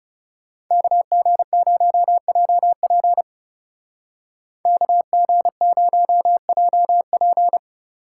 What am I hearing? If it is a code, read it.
Morse code, heard as a keyed tone, about 34 wpm, 700 Hz. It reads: KG0JP KG0JP
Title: Speed Racing - USA 2x2 Call Signs 30wpm